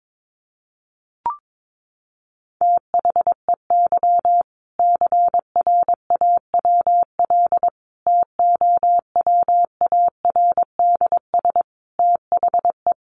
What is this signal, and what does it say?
Morse code, heard as a keyed tone, about 22 wpm, 700 Hz. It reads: THEY CRAWL TOWARDS THE
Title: Easy Phrases 22wpm